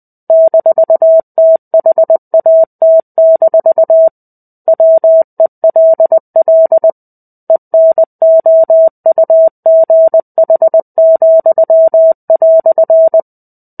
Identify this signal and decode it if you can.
Morse code, heard as a keyed tone, about 20 wpm, 650 Hz. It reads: -THAT- WELL ENOUGH,"
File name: 05-Chapter-05 20 WPM